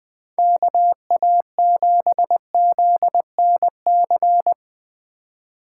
Morse code, heard as a keyed tone, about 20 wpm, 700 Hz. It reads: KA7ZNC